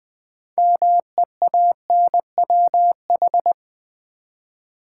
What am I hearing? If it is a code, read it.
Morse code, heard as a keyed tone, about 20 wpm, 700 Hz. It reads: MEANWH
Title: Build A Word - 3 Letters At A Time - 9-Letter Words - 20wpm